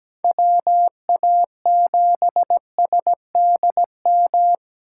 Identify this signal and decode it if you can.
Morse code, heard as a keyed tone, about 17 wpm, 700 Hz. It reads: WA7SDM
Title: USA Call Signs - 6 Characters - 4x Character Spacing - 17wpm